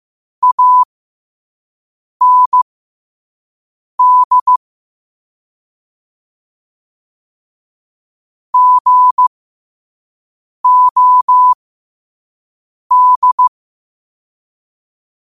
Morse code, heard as a keyed tone, about 15 wpm, spaced as for 5 wpm, 1000 Hz. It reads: AND GOD
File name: qotdfiles / 5wpm